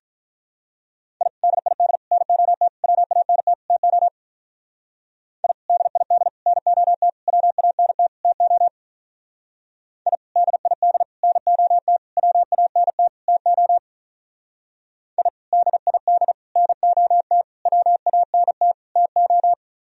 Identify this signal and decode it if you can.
Morse code, heard as a keyed tone, about 46 wpm, 700 Hz. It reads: I DID NOT WANT TO I DID NOT WANT TO I DID NOT WANT TO I DID NOT WANT TO